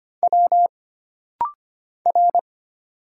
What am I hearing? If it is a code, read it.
Morse code, heard as a keyed tone, about 25 wpm, 700 Hz. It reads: W R
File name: Single Character (Letter:Number:Punctuation:Prosign) - Rapid-Fire - 25wpm